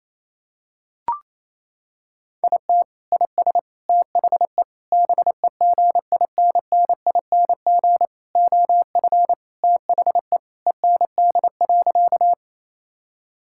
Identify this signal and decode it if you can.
Morse code, heard as a keyed tone, about 28 wpm, 700 Hz. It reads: IT IS THE BEGINNING OF THE END.
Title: Sentences from Top 1000 Words - Review Part 5 - 28wpm